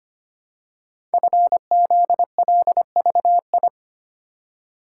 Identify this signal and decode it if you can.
Morse code, heard as a keyed tone, about 25 wpm, 700 Hz. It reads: FZLVI
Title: Code Groups - 5 Letters - 25wpm